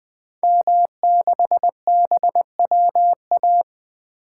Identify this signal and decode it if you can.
Morse code, heard as a keyed tone, about 20 wpm, 700 Hz. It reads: M6BWA
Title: 2023 SOTA Activator Call Signs - 20wpm